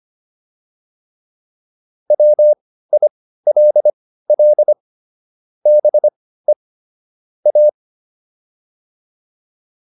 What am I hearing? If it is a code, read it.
Morse code, heard as a keyed tone, about 25 wpm, spaced as for 15 wpm, 600 Hz. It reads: WILL BE A